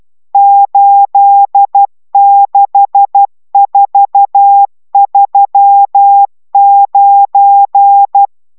Morse code, heard as a keyed tone, about 12 wpm, 800 Hz. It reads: 86439